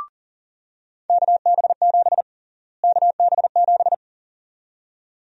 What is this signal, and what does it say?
Morse code, heard as a keyed tone, about 40 wpm, 700 Hz. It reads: KB7 KB7
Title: USA Call Sign Prefixes - Repeated 2x - 40wpm